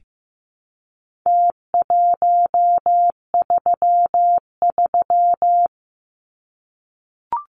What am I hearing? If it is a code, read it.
Morse code, heard as a keyed tone, about 15 wpm, 700 Hz. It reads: T133
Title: Numbers - Contest Style - 4-Digit 15wpm